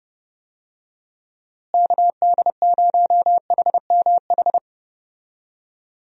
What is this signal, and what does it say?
Morse code, heard as a keyed tone, about 30 wpm, 700 Hz. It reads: KD0HMH
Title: USA Call Signs - 6 Characters - Repeated 2x - 30wpm